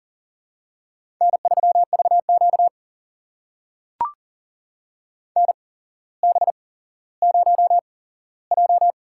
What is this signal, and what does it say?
Morse code, heard as a keyed tone, about 37 wpm, spaced as for 17 wpm, 700 Hz. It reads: N3VQ NB0J
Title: USA Call Signs - 4 Characters - 8x Character Spacing - 40wpm